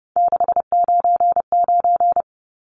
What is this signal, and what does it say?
Morse code, heard as a keyed tone, about 30 wpm, 700 Hz. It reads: 699
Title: Numbers - 3-Digit 30wpm